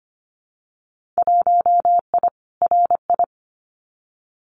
Morse code, heard as a keyed tone, about 25 wpm, 700 Hz. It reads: 1I RI
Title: Winter Field Day Exchanges Encore 25wpm